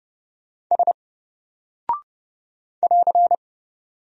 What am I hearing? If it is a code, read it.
Morse code, heard as a keyed tone, about 30 wpm, 700 Hz. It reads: S <AR>
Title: Single Character (Letter:Number:Punctuation:Prosign) Alt 30wpm